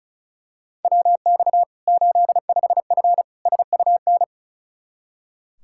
Morse code, heard as a keyed tone, about 36 wpm, 700 Hz. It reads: WX 85F SUN
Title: Practice QSO - Exchange 2 - Info - 35wpm